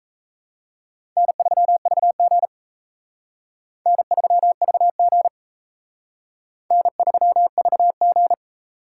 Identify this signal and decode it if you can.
Morse code, heard as a keyed tone, about 34 wpm, 700 Hz. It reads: N3VG N3VG N3VG
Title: Speed Racing - USA 1x2 Call Signs 28wpm